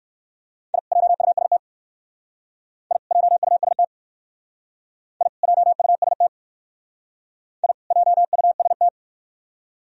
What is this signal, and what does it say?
Morse code, heard as a keyed tone, about 56 wpm, 700 Hz. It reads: I JUST I JUST I JUST I JUST